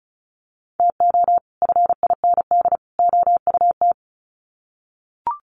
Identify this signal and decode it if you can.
Morse code, heard as a keyed tone, about 34 wpm, 700 Hz. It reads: TO FIND OUT
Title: Sets of 3 Words 35wpm